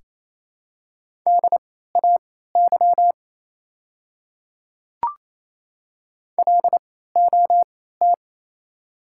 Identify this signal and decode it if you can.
Morse code, heard as a keyed tone, about 28 wpm, spaced as for 16 wpm, 700 Hz. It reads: DAY LOT
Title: 3x Character Spacing - Top 500 Words - 28wpm